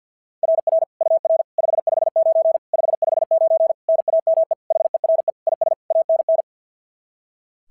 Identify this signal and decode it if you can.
Morse code, heard as a keyed tone, about 46 wpm, 690 Hz. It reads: RR UR 559 559 NAME HERE IS ANN
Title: SOTA QSO Practice 50wpm